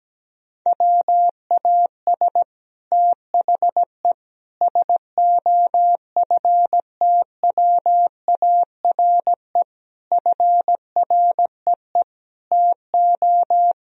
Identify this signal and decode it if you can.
Morse code, heard as a keyed tone, about 17 wpm, 700 Hz. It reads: WAS THE SOFTWARE FREE TO
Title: Sentences from Top 1000 Words - Review Part 3 - 17wpm